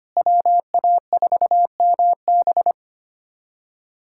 Morse code, heard as a keyed tone, about 25 wpm, 700 Hz. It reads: WA4MB
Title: USA Callsign DE Callsign - 5 Characters - 25wpm